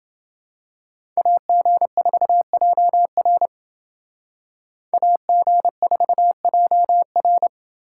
Morse code, heard as a keyed tone, about 29 wpm, 700 Hz. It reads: AG4JR AG4JR